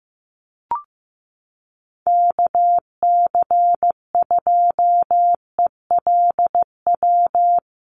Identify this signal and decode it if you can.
Morse code, heard as a keyed tone, about 15 wpm, 700 Hz. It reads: KC2ELW
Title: USA Call Signs Repeated 2x 15wpm